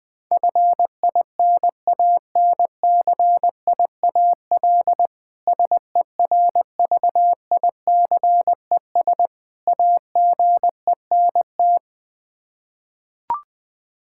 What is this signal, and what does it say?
Morse code, heard as a keyed tone, about 20 wpm, 700 Hz. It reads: FINANCIAL SERVICES AGENT
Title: Practice QSO - Exchange 3 - More Info - 20wpm